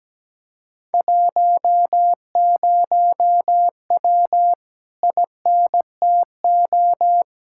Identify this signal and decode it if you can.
Morse code, heard as a keyed tone, about 17 wpm, 700 Hz. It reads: 10W INTO
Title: Speed-Racing Common QSO Phrases - 17wpm